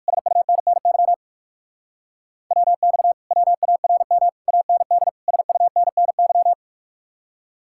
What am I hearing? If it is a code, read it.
Morse code, heard as a keyed tone, about 66 wpm, spaced as for 56 wpm, 700 Hz. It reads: SUNNY WX WARM AND SUNNY
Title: Speed-Racing Common QSO Phrases - 45wpm